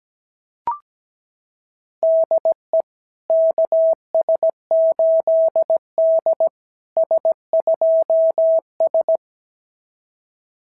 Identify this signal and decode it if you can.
Morse code, heard as a keyed tone, about 17 wpm, 660 Hz. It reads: DE KS8D S2S
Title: SOTA QSO Practice 17wpm